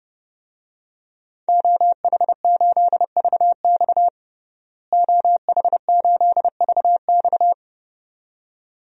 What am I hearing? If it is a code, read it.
Morse code, heard as a keyed tone, about 30 wpm, 700 Hz. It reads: OH8VX OH8VX
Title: DXCC-100 Call Signs - 5 Characters - Repeated 2x - 30wpm